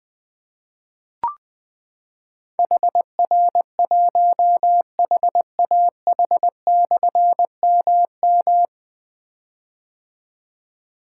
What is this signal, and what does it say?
Morse code, heard as a keyed tone, about 20 wpm, 700 Hz. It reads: HR1HAH/MM
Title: Difficult Call Signs 20wpm